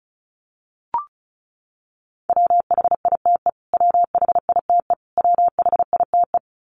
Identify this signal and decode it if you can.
Morse code, heard as a keyed tone, about 34 wpm, 700 Hz. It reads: WHITE WHITE WHITE
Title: Top 500 Words Repeated 35wpm